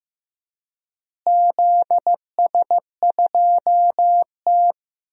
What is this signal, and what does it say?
Morse code, heard as a keyed tone, about 15 wpm, 700 Hz. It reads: ZS2T
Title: DXCC-100 Call Signs - 4 Characters - Repeated 2x - 15wpm